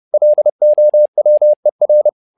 Morse code, heard as a keyed tone, about 30 wpm, 600 Hz. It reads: LOWER
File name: learning center / tower trouble / tower trouble 30wpm